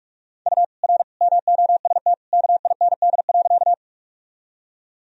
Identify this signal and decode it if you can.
Morse code, heard as a keyed tone, about 46 wpm, 700 Hz. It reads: U R MOST KIND.
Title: Sentences from Top 200 Words and Abbreviations 45wpm